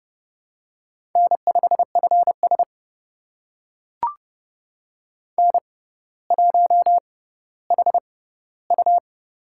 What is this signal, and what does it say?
Morse code, heard as a keyed tone, about 30 wpm, spaced as for 15 wpm, 700 Hz. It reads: N5FS N1HU
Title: USA Call Signs - 4 Characters - 6x Character Spacing - 30wpm